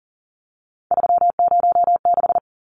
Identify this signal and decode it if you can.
Morse code, heard as a keyed tone, about 39 wpm, 700 Hz. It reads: 306
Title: Numbers - 3-Digit 40wpm